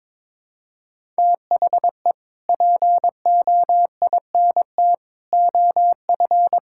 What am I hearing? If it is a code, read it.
Morse code, heard as a keyed tone, about 22 wpm, 700 Hz. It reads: THE POINT OF
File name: practice / 22wpm / Build A Phrase - 5-Word Phrases - 22wpm